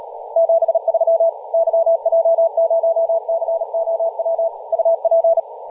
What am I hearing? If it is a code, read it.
Morse code, heard as a keyed tone, about 38 wpm, 660 Hz, with radio noise. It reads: 73 YJ0COW UP
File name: posts / 2016-aug-26-2236z-yj0cow-18-073mhz-cw